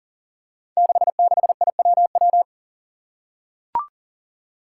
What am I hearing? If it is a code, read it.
Morse code, heard as a keyed tone, about 41 wpm, 700 Hz. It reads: 66IWW
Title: Five Challenging Letter-Number Combinations 40wpm